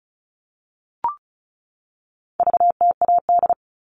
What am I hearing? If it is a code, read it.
Morse code, heard as a keyed tone, about 34 wpm, 700 Hz. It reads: VTAD
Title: All Cut Numbers - 4-Digit 35wpm